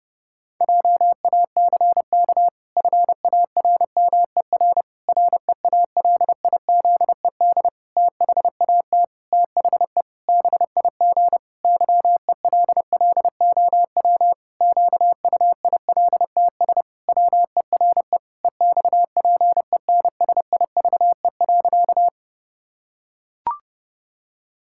Morse code, heard as a keyed tone, about 30 wpm, 700 Hz. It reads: JACK FARMER REALIZED THAT THE BIG YELLOW QUILTS WERE EXPENSIVE.
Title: Pangram Challenge v2 - 30wpm